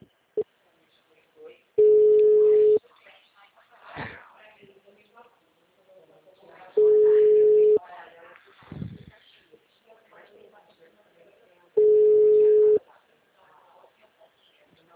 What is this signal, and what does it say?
Ringback tone (Europe, 425 Hz), with speech.